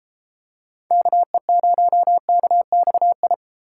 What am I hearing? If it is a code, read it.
Morse code, heard as a keyed tone, about 33 wpm, 700 Hz. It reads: KE0KXI